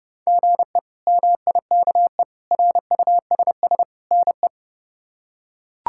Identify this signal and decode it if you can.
Morse code, heard as a keyed tone, about 30 wpm, 700 Hz. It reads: GE MIKE RUSS NE
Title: K1USN SST QSO Practice - 30wpm